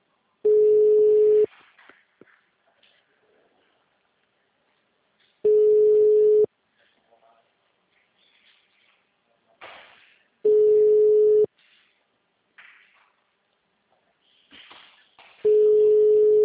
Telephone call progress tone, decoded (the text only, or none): ringback tone